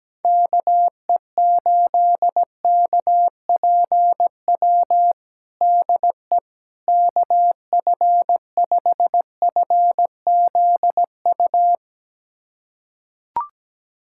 Morse code, heard as a keyed tone, about 17 wpm, 700 Hz. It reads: KE8KPW DE KF5FZU
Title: USA Callsign DE Callsign 17wpm